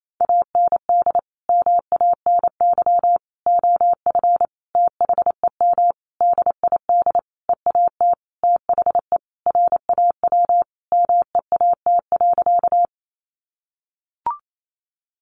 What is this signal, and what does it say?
Morse code, heard as a keyed tone, about 28 wpm, 700 Hz. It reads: AND MANY OF THEM DID EAT THE RAW MEAT.
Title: Sentences from Top 1500 Words 28wpm